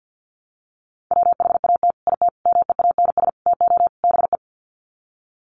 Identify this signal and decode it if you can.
Morse code, heard as a keyed tone, about 48 wpm, 700 Hz. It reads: WHAT IT MEANS TO BE
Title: Sets of 5 Words 50wpm